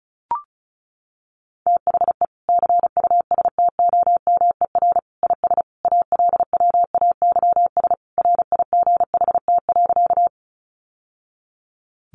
Morse code, heard as a keyed tone, about 34 wpm, 700 Hz. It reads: THE CUSTOMER IS ALWAYS RIGHT.
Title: Sentences from Top 1100 Words 35wpm